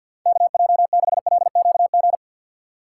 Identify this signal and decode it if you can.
Morse code, heard as a keyed tone, about 52 wpm, 700 Hz. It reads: KJ6LQG
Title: USA Call Signs - 6 Characters - 4x Character Spacing - 50wpm